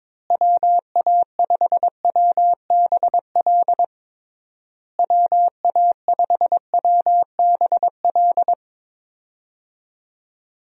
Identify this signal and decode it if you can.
Morse code, heard as a keyed tone, about 22 wpm, 700 Hz. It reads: WA5WBL WA5WBL
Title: USA Call Signs - 6 Characters - Repeated 3x - 22wpm